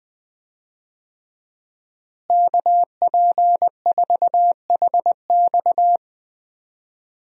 Morse code, heard as a keyed tone, about 20 wpm, 700 Hz. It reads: KP4HX
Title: USA Call Signs - 5 Characters - Repeated 3x - 20wpm